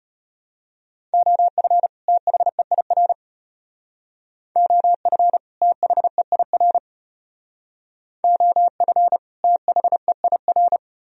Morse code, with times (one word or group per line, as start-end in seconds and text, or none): 1.1-1.9 OF
2.1-3.1 THEIR
4.6-5.4 OF
5.6-6.8 THEIR
8.2-9.2 OF
9.4-10.8 THEIR